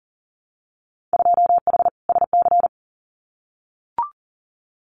Morse code, heard as a keyed tone, about 41 wpm, 700 Hz. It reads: 2H SC
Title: Winter Field Day Exchanges 40wpm